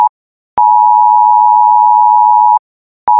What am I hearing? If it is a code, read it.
Emergency Alert System (SAME) tones (the 853 and 960 Hz attention tone). It is heard from 0.5 to 2.6 s.
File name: PhoneEasy-626 / System / Emergency Alert